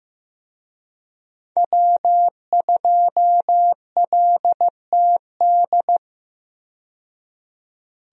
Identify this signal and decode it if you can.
Morse code, heard as a keyed tone, about 15 wpm, 700 Hz. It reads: W2LTD
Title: USA Call Signs - 5 Characters - 15wpm